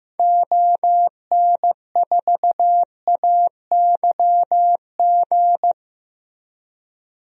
Morse code, heard as a keyed tone, about 15 wpm, 700 Hz. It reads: ON4AYG
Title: DXCC-100 Call Signs - 6 Characters - Repeated 2x - 15wpm